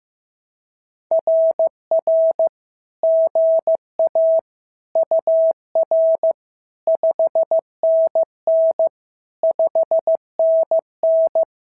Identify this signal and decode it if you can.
Morse code, heard as a keyed tone, about 15 wpm, 650 Hz. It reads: RR GA UR 5NN 5NN
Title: SOTA QSO Practice 15wpm